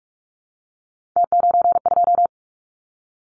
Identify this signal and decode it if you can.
Morse code, heard as a keyed tone, about 44 wpm, 700 Hz. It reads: T92
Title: Callsign Prefixes for Top 100 DXCC - 45wpm